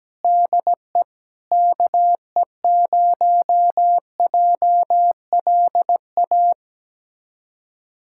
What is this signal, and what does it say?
Morse code, heard as a keyed tone, about 17 wpm, 700 Hz. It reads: DE KE0JLA
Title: USA Callsign DE Callsign - 6 Characters - 17wpm